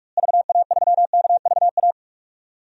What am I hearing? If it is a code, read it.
Morse code, heard as a keyed tone, about 48 wpm, 700 Hz. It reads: VA3KVA
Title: DXCC-100 Call Signs - 6 Characters - Repeated 2x - 45wpm